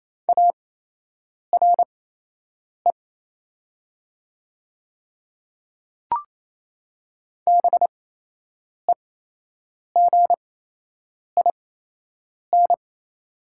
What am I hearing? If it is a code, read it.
Morse code, heard as a keyed tone, about 28 wpm, spaced as for 8 wpm, 700 Hz. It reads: ARE BEGIN
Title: 8x Character Spacing - Top 300 Words - 28wpm